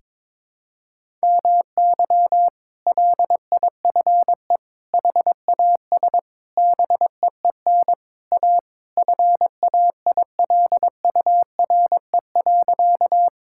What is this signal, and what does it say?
Morse code, heard as a keyed tone, about 21 wpm, 700 Hz. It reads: MY LIFE HAS BEEN A FAILURE.
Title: Sentences from Top 1000 Words - Review Part 6 - 22wpm